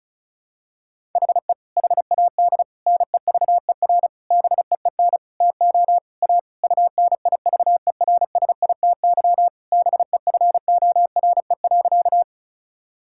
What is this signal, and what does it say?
Morse code, heard as a keyed tone, about 34 wpm, 700 Hz. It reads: HE HAD NEVER BEEN TO A UNIVERSITY BEFORE.
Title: Sentences from Top 1000 Words - Review Part 5 - 35wpm